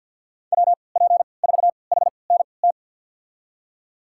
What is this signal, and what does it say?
Morse code, heard as a keyed tone, about 51 wpm, spaced as for 28 wpm, 700 Hz. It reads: WP4HNT
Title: USA Call Signs - 3x Character Spacing - 50wpm